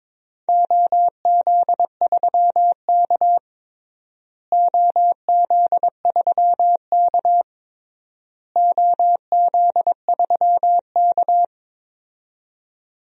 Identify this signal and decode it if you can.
Morse code, heard as a keyed tone, about 22 wpm, 700 Hz. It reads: OZ3K OZ3K OZ3K
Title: DXCC-100 Call Signs - 4 Characters - Repeated 3x - 22wpm